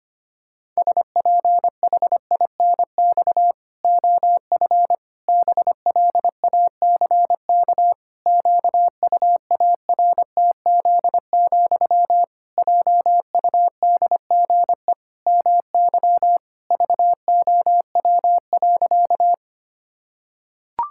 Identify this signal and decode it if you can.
Morse code, heard as a keyed tone, about 25 wpm, 700 Hz. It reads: SPHINX OF BLACK QUARTZ, JUDGE MY VOW.
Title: Pangram Challenge v2 - 25wpm